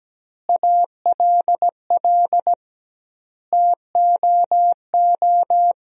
Morse code, heard as a keyed tone, about 17 wpm, 700 Hz. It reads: ALL TOO
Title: 2x Word-Spacing - Sentences from Top 300 Words - 17wpm